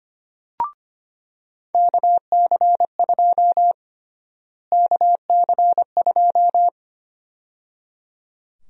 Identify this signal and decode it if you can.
Morse code, heard as a keyed tone, about 25 wpm, 700 Hz. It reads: KC2 KC2
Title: USA Call Sign Prefixes - Repeated 2x - 25wpm